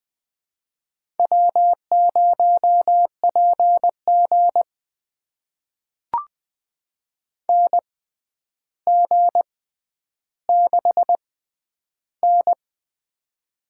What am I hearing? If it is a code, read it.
Morse code, heard as a keyed tone, about 20 wpm, spaced as for 10 wpm, 700 Hz. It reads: W0PG NG6N
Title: USA Call Signs - 4 Characters - 6x Character Spacing - 20wpm